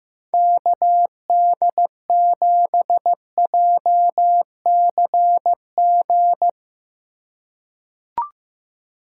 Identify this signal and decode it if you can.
Morse code, heard as a keyed tone, about 15 wpm, 700 Hz. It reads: KD7JCG
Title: USA Call Signs - 6 Characters - 3x Character Spacing - 15wpm